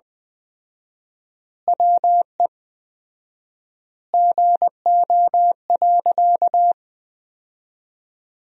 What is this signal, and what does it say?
Morse code, heard as a keyed tone, about 20 wpm, 700 Hz. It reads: WE GO.
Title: 4x Word-Spacing - Sentences from Top 200 Words - 20wpm